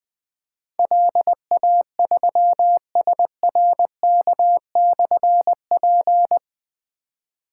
Morse code, heard as a keyed tone, about 20 wpm, 700 Hz. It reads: LA3SRK/P
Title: Difficult Call Signs 20wpm